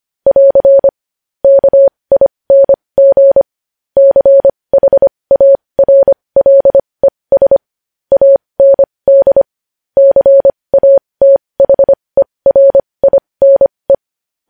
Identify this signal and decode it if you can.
Morse code, heard as a keyed tone, about 25 wpm, spaced as for 20 wpm, 550 Hz. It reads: <AR> KING CHARLES AND CATHERINE
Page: News Headlines in Morse Code at 20 WPM